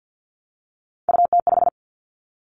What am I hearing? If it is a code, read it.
Morse code, heard as a keyed tone, about 50 wpm, 700 Hz. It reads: UT5
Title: Callsign Prefixes for Top 25 DXCC - 50wpm